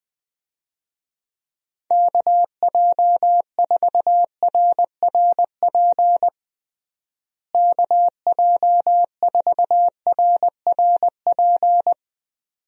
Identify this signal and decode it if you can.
Morse code, heard as a keyed tone, about 20 wpm, 700 Hz. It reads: KJ4RRP KJ4RRP
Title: USA Call Signs - 6 Characters - Repeated 2x - 20wpm